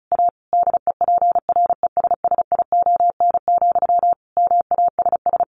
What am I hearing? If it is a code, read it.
Morse code, heard as a keyed tone, about 34 wpm, 700 Hz. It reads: A DEPRESSION, MASS
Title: A Sound Of Thunder 35wpm